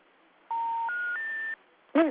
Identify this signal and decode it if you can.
Special information tone.